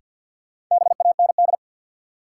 Morse code, heard as a keyed tone, about 50 wpm, 700 Hz. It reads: BAND
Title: Sentences from Top 1600 Words - New Word Drill Alt - 50wpm